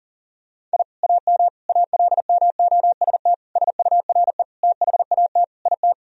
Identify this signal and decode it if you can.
Morse code, heard as a keyed tone, about 40 wpm, 700 Hz. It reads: I AM ALMOST SURE THAT IT
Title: Sentences from Top 1000 Words - Review Part 4 - 40wpm